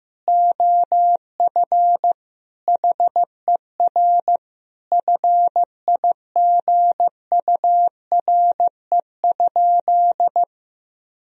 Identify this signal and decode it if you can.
Morse code, heard as a keyed tone, about 15 wpm, 700 Hz. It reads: OF HER FIGURE?
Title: Sentences from Top 1000 Words - Review Part 2 - 15wpm